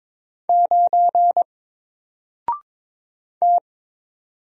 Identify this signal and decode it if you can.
Morse code, heard as a keyed tone, about 22 wpm, 700 Hz. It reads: 9 T
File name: Single-Letter-Number-22wpm